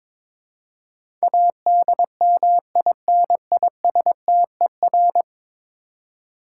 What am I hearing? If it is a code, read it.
Morse code, heard as a keyed tone, about 21 wpm, 700 Hz. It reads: ADMINISTER
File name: practice / 22wpm / 10 Letter Words - First 3 Letters Given - 22wpm